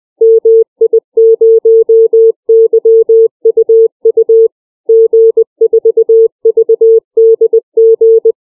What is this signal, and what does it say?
Morse code, heard as a keyed tone, about 20 wpm, 450 Hz. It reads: MI0YUU G4VDG
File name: CW / callsign